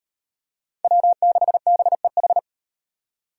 Morse code, heard as a keyed tone, about 36 wpm, 700 Hz. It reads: W6BEH